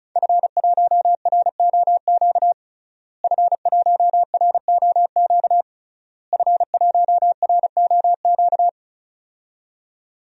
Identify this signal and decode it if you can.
Morse code, heard as a keyed tone, about 34 wpm, 700 Hz. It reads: F1ROQ F1ROQ F1ROQ
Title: DXCC-100 Call Signs - 5 Characters - Repeated 3x - 35wpm